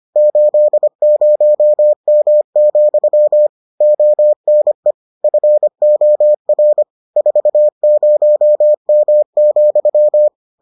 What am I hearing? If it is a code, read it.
Morse code, heard as a keyed tone, about 25 wpm, 600 Hz. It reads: 80M, ONE FOR 40M,